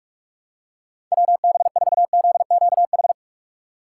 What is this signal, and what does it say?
Morse code, heard as a keyed tone, about 46 wpm, 700 Hz. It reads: WB4ZQH
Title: USA Call Signs - 6 Characters - 3x Character Spacing - 45wpm